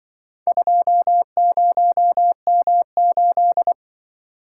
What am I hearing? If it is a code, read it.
Morse code, heard as a keyed tone, about 25 wpm, 700 Hz. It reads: 20M8
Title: Speed-Racing - Four Letter-Number Combinations - 20wpm